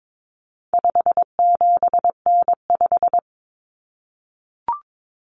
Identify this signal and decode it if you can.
Morse code, heard as a keyed tone, about 22 wpm, 700 Hz. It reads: 57N5
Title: Common Cut Numbers - 4-Digit 22wpm